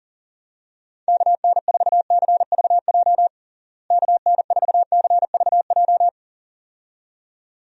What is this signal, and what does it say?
Morse code, heard as a keyed tone, about 37 wpm, 700 Hz. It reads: KN4CVJ KN4CVJ
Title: USA Call Signs - 6 Characters - Repeated 2x - 40wpm